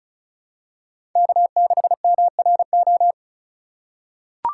Morse code, heard as a keyed tone, about 36 wpm, 700 Hz. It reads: K6MRO
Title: USA Call Signs - 5 Characters - 4x Character Spacing - 35wpm